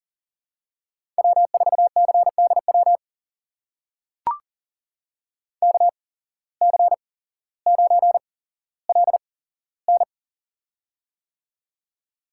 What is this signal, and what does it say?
Morse code, heard as a keyed tone, about 41 wpm, spaced as for 17 wpm, 700 Hz. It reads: W4CDW KC9LN
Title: USA Call Signs - 5 Characters - 8x Character Spacing - 40wpm